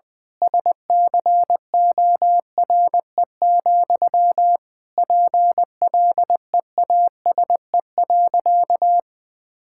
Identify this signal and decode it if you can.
Morse code, heard as a keyed tone, about 20 wpm, 700 Hz. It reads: SCORE, PLEASE.
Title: Sentences from Top 1600 Words 20wpm